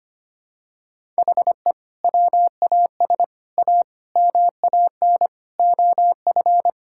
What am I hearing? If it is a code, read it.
Morse code, heard as a keyed tone, about 25 wpm, 700 Hz. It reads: HE WAS A MAN OF
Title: Sentences from Top 1000 Words - Review Part 6 - 25wpm